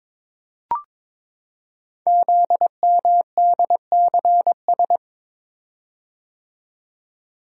Morse code, heard as a keyed tone, about 22 wpm, 700 Hz. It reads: ZMDCS
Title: Code Groups - 5 Letters - 22wpm